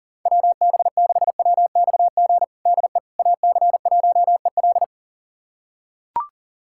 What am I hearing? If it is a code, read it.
Morse code, heard as a keyed tone, about 37 wpm, 700 Hz. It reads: WB6WXG DE AC1EL
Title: USA Callsign DE Callsign 40wpm